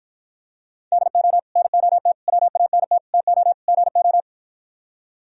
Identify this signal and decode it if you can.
Morse code, heard as a keyed tone, about 52 wpm, 700 Hz. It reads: DO NOT WANT TO GO